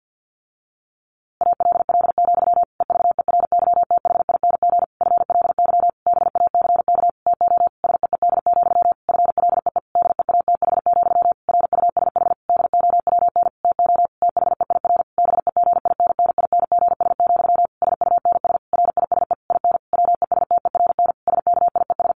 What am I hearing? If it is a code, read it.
Morse code, heard as a keyed tone, about 50 wpm, 700 Hz. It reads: ALL, EVERYTHING FLY BACK TO SEED, FLEE DEATH, RUSH DOWN TO THEIR BEGINNINGS, SUNS RISE IN WESTERN SKIES
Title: A Sound Of Thunder 50wpm